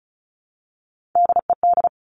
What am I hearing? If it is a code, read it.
Morse code, heard as a keyed tone, about 36 wpm, 700 Hz. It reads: DED
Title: Top Three Letter Combinations in Words 35wpm